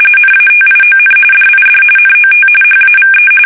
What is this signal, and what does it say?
RTTY, about 45.45 baud. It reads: NATION DID NOT HAVE
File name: fsk850 / usnavy1